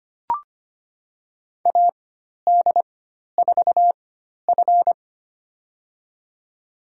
Morse code, heard as a keyed tone, about 25 wpm, spaced as for 12 wpm, 690 Hz. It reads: AD4F
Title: USA Call Signs - 4 Characters - 4x Character Spacing - 25wpm